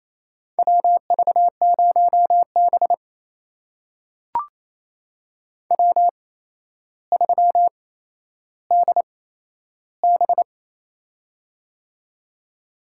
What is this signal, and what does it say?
Morse code, heard as a keyed tone, about 27 wpm, spaced as for 12 wpm, 700 Hz. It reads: WV0B W3DB
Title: USA Call Signs - 4 Characters - 8x Character Spacing - 28wpm